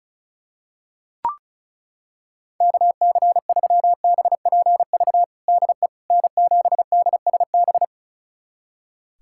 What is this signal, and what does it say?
Morse code, heard as a keyed tone, about 36 wpm, 700 Hz. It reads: KC3BPV DE N7DSB
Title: USA Callsign DE Callsign 35wpm